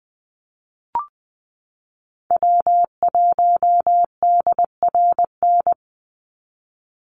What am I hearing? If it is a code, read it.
Morse code, heard as a keyed tone, about 20 wpm, 700 Hz. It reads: W1DRN
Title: USA Call Signs 20wpm